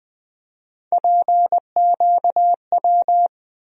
Morse code, heard as a keyed tone, about 20 wpm, 700 Hz. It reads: PQW
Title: USA Call Sign Postfixes - 3 Letters 20wpm